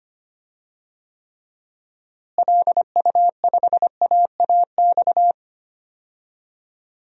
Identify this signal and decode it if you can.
Morse code, heard as a keyed tone, about 25 wpm, 700 Hz. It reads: LU5AAX
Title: DXCC-100 Call Signs - 6 Characters - Repeated 3x - 25wpm